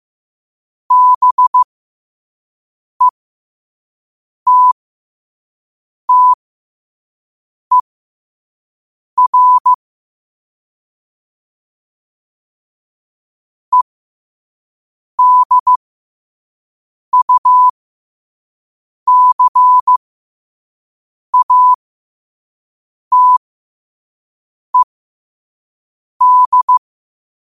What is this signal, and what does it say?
Morse code, heard as a keyed tone, about 15 wpm, spaced as for 5 wpm, 1000 Hz. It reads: BETTER EDUCATED
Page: New quotes every day in morse code at 5 Words per minute.